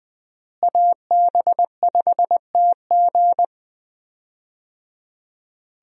Morse code, heard as a keyed tone, about 20 wpm, 700 Hz. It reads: AB5TG
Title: USA Call Signs - 5 Characters - Repeated 3x - 20wpm